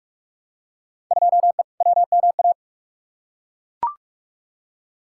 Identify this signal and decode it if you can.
Morse code, heard as a keyed tone, about 41 wpm, 700 Hz. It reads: 2E WMA
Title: Field Day Exchanges Encore - 45wpm